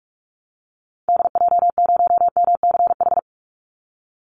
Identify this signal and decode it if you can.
Morse code, heard as a keyed tone, about 44 wpm, 700 Hz. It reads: DJ0MCH
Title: Call Signs Repeated 3x - 45wpm